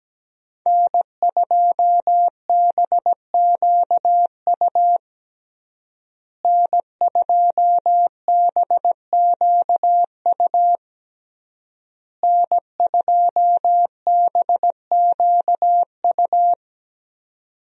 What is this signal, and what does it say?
Morse code, heard as a keyed tone, about 17 wpm, 700 Hz. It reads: N2BQU N2BQU N2BQU
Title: USA Call Signs - 5 Characters - Repeated 3x - 17wpm